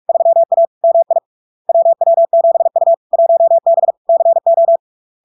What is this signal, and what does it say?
Morse code, heard as a keyed tone, about 46 wpm, 670 Hz. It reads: 3A MI WW7U 1B CO
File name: LARK-FD-2022-Log-in-CW-45wpm-670-Hz